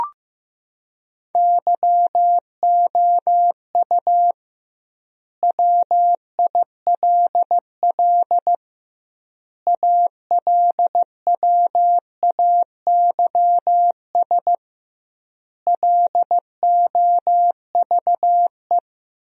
Morse code, heard as a keyed tone, about 15 wpm, 700 Hz. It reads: YOU WILL ALWAYS LOVE
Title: 2x Word-Spacing - Sentences from Top 400 Words - 15wpm